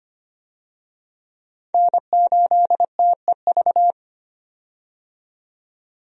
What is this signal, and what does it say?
Morse code, heard as a keyed tone, about 25 wpm, 700 Hz. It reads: N8TEV
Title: USA Call Signs - 5 Characters - 25wpm